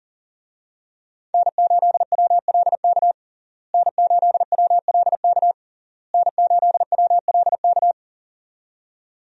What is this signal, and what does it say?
Morse code, heard as a keyed tone, about 40 wpm, 700 Hz. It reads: N8WLK N8WLK N8WLK
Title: USA Call Signs - 5 Characters - Repeated 3x - 40wpm